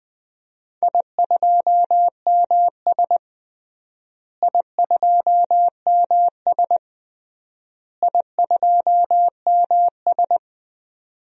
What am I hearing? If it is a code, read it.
Morse code, heard as a keyed tone, about 20 wpm, 700 Hz. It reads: I2MS I2MS I2MS
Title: DXCC-100 Call Signs - 4 Characters - Repeated 3x - 20wpm